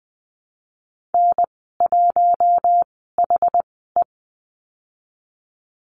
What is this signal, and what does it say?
Morse code, heard as a keyed tone, about 20 wpm, spaced as for 15 wpm, 700 Hz. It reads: N1HE
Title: 088 Lesson 15g - Number 5 - Call Signs 20wpm